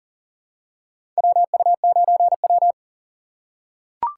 Morse code, heard as a keyed tone, about 40 wpm, 700 Hz. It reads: WU9W
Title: USA Call Signs - 4 Characters - 8x Character Spacing - 40wpm